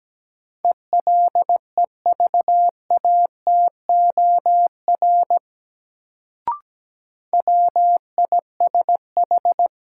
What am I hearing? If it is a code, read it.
Morse code, heard as a keyed tone, about 17 wpm, 700 Hz. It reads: ELEVATOR WISH
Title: Sentences from Top 1600 Words - New Word Drill - 17wpm